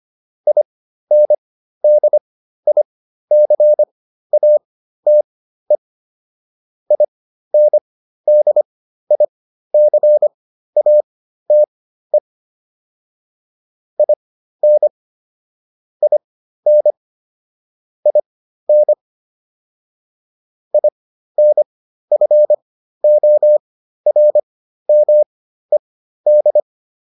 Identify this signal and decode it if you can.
Morse code, heard as a keyed tone, about 25 wpm, spaced as for 13 wpm, 600 Hz. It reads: INDICATE INDICATE IN IN IN INFORMED